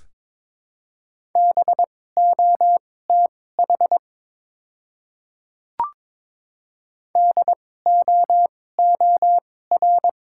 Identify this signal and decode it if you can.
Morse code, heard as a keyed tone, about 22 wpm, spaced as for 16 wpm, 700 Hz. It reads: BOTH DOOR
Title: 2x Character Spacing - Top 300 Words - 22wpm